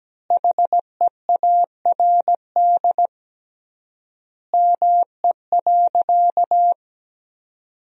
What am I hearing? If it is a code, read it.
Morse code, heard as a keyed tone, about 17 wpm, 700 Hz. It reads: HEARD ME.
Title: 3x Word-Spacing - Sentences from Top 400 Words - 17wpm